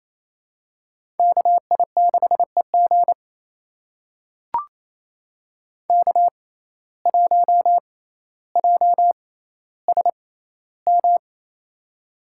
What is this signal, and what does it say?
Morse code, heard as a keyed tone, about 28 wpm, spaced as for 14 wpm, 700 Hz. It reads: KI6EG K1JSM
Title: USA Call Signs - 5 Characters - 6x Character Spacing - 28wpm